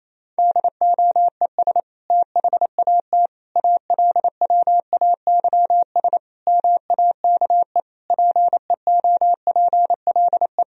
Morse code, heard as a keyed tone, about 28 wpm, 700 Hz. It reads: DOES THAT ALWAYS MAKE PEOPLE
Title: Sentences from Top 1000 Words - Review Part 2 - 28wpm